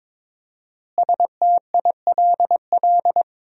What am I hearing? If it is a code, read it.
Morse code, heard as a keyed tone, about 22 wpm, 700 Hz. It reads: STILL
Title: 4x Word-Spacing - Sentences from Top 400 Words - 22wpm